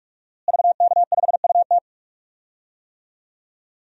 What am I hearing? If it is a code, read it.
Morse code, heard as a keyed tone, about 42 wpm, 700 Hz. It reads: VK5UT
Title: DXCC-100 Call Signs - 5 Characters - Repeated 2x - 45wpm